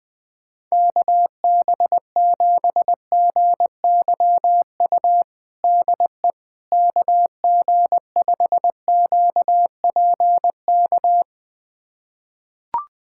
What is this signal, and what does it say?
Morse code, heard as a keyed tone, about 21 wpm, 700 Hz. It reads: KB7GYU DE KG5QPK
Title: USA Callsign DE Callsign - 6 Characters - 20wpm